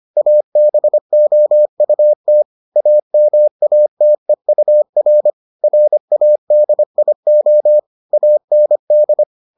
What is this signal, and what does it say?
Morse code, heard as a keyed tone, about 25 wpm, 600 Hz. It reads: ABOUT AMATEUR RADIO AND